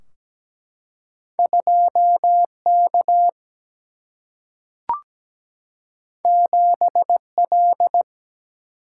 Morse code, heard as a keyed tone, about 17 wpm, 700 Hz. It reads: 2 K 7 L
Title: Single-Letter Single-Number 17wpm